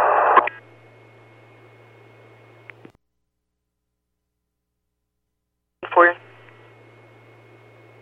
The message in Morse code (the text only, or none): none